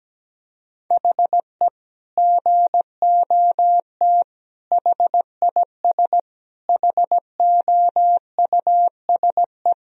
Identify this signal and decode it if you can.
Morse code, heard as a keyed tone, about 17 wpm, 700 Hz. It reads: HE GOT HIS HOUSE
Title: Sentences from Top 1000 Words - Review Part 1 - 17wpm